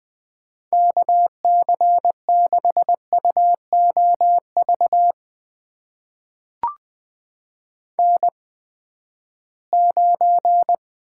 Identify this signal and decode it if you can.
Morse code, heard as a keyed tone, about 20 wpm, spaced as for 14 wpm, 700 Hz. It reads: KC6UOV N9
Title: USA Call Signs - 8x Character Spacing - 20wpm